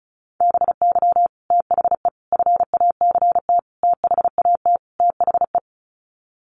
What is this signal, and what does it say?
Morse code, heard as a keyed tone, about 34 wpm, 700 Hz. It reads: BY THE FACT THAT THE
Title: Sets of 5 Words Encore 35wpm